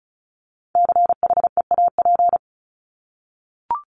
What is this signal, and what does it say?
Morse code, heard as a keyed tone, about 34 wpm, 700 Hz. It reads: CHEAP
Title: Top 1001-1250 Words 35wpm